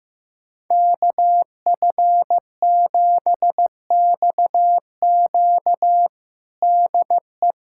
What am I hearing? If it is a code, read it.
Morse code, heard as a keyed tone, about 15 wpm, 700 Hz. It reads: KF7XQ DE
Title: USA Callsign DE Callsign - 5 Characters - 15wpm